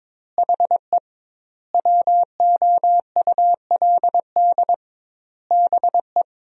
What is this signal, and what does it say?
Morse code, heard as a keyed tone, about 22 wpm, 700 Hz. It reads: HE WOULD BE
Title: 2x Word-Spacing - Sentences from Top 300 Words - 22wpm